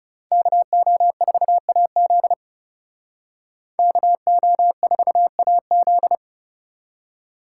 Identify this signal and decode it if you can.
Morse code, heard as a keyed tone, about 36 wpm, 700 Hz. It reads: KO4AZ KO4AZ
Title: Speed Racing - USA Call Signs 30wpm